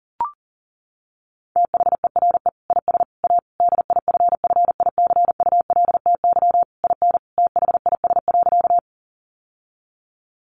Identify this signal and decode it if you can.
Morse code, heard as a keyed tone, about 40 wpm, 700 Hz. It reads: THERE IS A DIFFICULTY IN THIS.
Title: Sentences from Top 1500 Words 40wpm